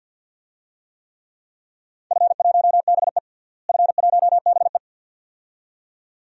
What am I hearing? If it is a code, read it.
Morse code, heard as a keyed tone, about 49 wpm, 700 Hz. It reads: F1BE F1BE
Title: DXCC-100 Call Signs - 4 Characters - Repeated 2x - 50wpm